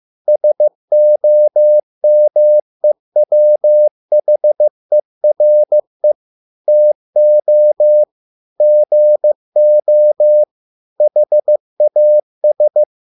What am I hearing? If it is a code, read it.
Morse code, heard as a keyed tone, about 15 wpm, 600 Hz. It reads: SOMEWHERE TO GO HAS